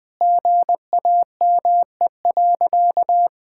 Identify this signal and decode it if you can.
Morse code, heard as a keyed tone, about 21 wpm, 700 Hz. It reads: GAME.
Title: Sentences with 6-Word Phrases - Spoken Beginning - 20wpm